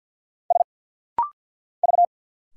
Morse code, heard as a keyed tone, about 50 wpm, 700 Hz. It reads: S V
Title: Single Letters - Rapid-Fire 50wpm